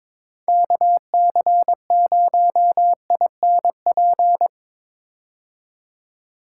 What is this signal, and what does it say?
Morse code, heard as a keyed tone, about 22 wpm, 700 Hz. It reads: KC0INP
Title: K1USN SST QSO Practice - 22wpm